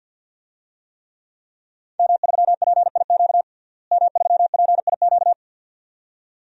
Morse code, heard as a keyed tone, about 52 wpm, 700 Hz. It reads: M3PIQ M3PIQ
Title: DXCC-100 Call Signs - 5 Characters - Repeated 2x - 50wpm